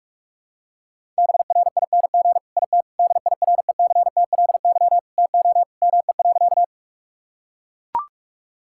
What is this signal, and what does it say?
Morse code, heard as a keyed tone, about 41 wpm, 700 Hz. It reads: BRING IT DIRECTLY TO ME.
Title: Sentences from Top 1000 Words - Review Part 5 - 45wpm